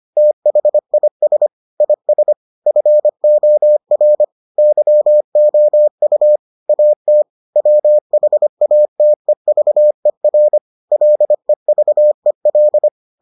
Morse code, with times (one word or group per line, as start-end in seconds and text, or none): 0.2-1.5 THIS
1.8-2.3 IS
2.7-4.2 FOR
4.6-6.4 YOU
6.7-7.2 AT
7.6-10.6 WHATEVER
10.9-12.9 LEVEL